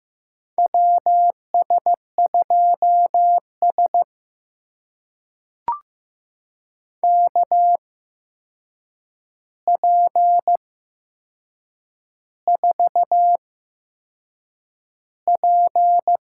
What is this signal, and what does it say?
Morse code, heard as a keyed tone, about 15 wpm, spaced as for 6 wpm, 700 Hz. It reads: WS2S KP4P
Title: USA Call Signs - 4 Characters - 8x Character Spacing - 15wpm